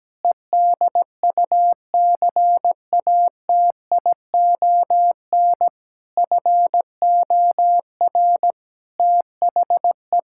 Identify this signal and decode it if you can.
Morse code, heard as a keyed tone, about 17 wpm, 700 Hz. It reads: EDUCATION FOR THE
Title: Sentences from Top 1000 Words - Review Part 6 - 17wpm